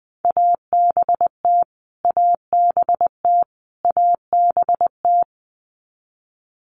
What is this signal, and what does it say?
Morse code, heard as a keyed tone, about 20 wpm, 700 Hz. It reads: ABT ABT ABT
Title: Abbreviations Repeated 3x - 20wpm